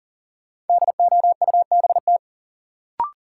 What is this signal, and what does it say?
Morse code, heard as a keyed tone, about 40 wpm, 700 Hz. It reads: DOUBT
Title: Sentences from Top 1500 Words - New Word Drill Alt - 40wpm